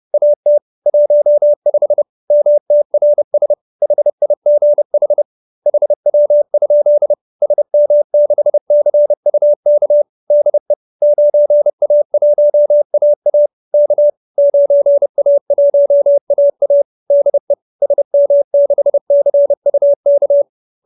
Morse code, heard as a keyed tone, about 30 wpm, 590 Hz. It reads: AT 15 MTRS HIGH HW? SM6CUK DE 9A1AA K 9A1AA DE SM6CUK
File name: QSO319 30